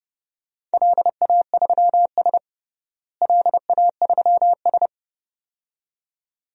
Morse code, heard as a keyed tone, about 31 wpm, 700 Hz. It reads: LA3S LA3S
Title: DXCC-100 Call Signs - 4 Characters - Repeated 2x - 30wpm